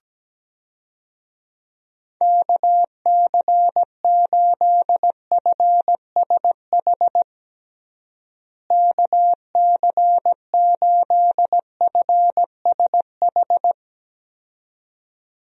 Morse code, heard as a keyed tone, about 17 wpm, 700 Hz. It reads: KC8FSH KC8FSH
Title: USA Call Signs - 6 Characters - Repeated 2x - 17wpm